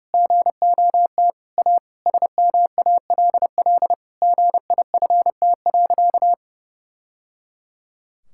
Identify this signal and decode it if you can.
Morse code, heard as a keyed tone, about 30 wpm, 700 Hz. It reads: GOT A SMALL GIFT.
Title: Sentences from Top 1000 Words - Review Part 6 - 30wpm